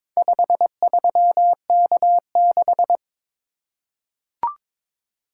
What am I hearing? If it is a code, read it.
Morse code, heard as a keyed tone, about 22 wpm, 690 Hz. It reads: 53K6
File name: practice / 22wpm / Four Letter-Number Combinations - 22wpm